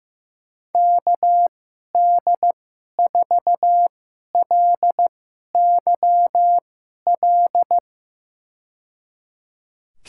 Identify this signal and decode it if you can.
Morse code, heard as a keyed tone, about 15 wpm, spaced as for 11 wpm, 700 Hz. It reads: KD4LYL
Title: USA Call Signs - 2x Character Spacing - 15wpm